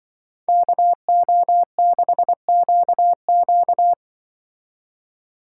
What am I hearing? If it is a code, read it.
Morse code, heard as a keyed tone, about 23 wpm, 700 Hz. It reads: KO6QQ
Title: Speed Racing - USA 2x2 Call Signs 20wpm